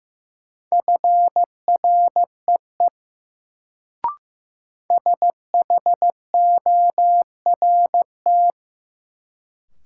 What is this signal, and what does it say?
Morse code, heard as a keyed tone, about 15 wpm, 700 Hz. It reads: FREE SHORT
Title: Sets of 5 Words Encore - New Word Drill - 15wpm